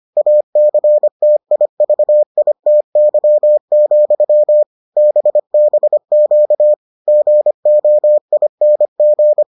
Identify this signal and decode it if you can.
Morse code, heard as a keyed tone, about 25 wpm, 600 Hz. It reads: ACTIVITY, BBQ GOING